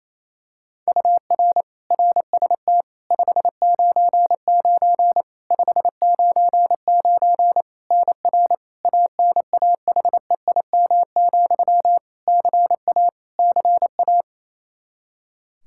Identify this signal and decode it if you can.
Morse code, heard as a keyed tone, about 28 wpm, 700 Hz. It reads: UR RST 599 599 NR ANAHEIM, CA CA
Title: Practice QSO - Exchange 1 - Beginning - 28wpm